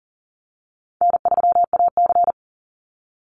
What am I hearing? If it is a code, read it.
Morse code, heard as a keyed tone, about 40 wpm, 700 Hz. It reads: N3AC
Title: Easy Call Signs Repeated 3x - 40wpm